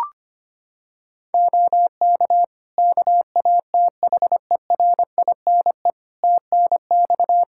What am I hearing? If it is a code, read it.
Morse code, heard as a keyed tone, about 25 wpm, 700 Hz. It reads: OK KATHERINE TNX
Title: K1USN SST QSO Practice 25wpm